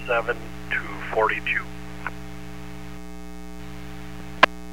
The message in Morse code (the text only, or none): none